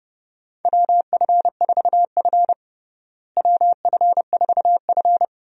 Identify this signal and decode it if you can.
Morse code, heard as a keyed tone, about 31 wpm, 700 Hz. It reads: WF4F WF4F
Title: USA Call Signs - 4 Characters - Repeated 3x - 30wpm